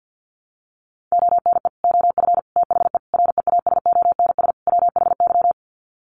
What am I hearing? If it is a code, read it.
Morse code, heard as a keyed tone, about 50 wpm, 700 Hz. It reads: ONE OF THE REASONS WHY
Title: Sets of 5 Words 50wpm